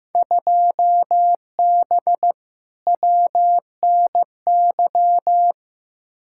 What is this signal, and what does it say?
Morse code, heard as a keyed tone, about 15 wpm, 700 Hz. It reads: 2B WNY
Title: Field Day Exchanges Encore - 15wpm